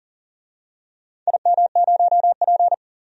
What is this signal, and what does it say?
Morse code, heard as a keyed tone, about 40 wpm, 700 Hz. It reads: IM0P
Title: DXCC-100 Call Signs - 4 Characters - Repeated 3x - 40wpm